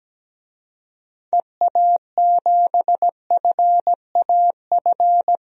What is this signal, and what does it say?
Morse code, heard as a keyed tone, about 17 wpm, 700 Hz. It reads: EA7FAF
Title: DXCC-100 Call Signs - 6 Characters - Repeated 3x - 17wpm